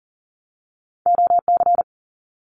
Morse code, heard as a keyed tone, about 40 wpm, 700 Hz. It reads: OC
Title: Abbreviations 40wpm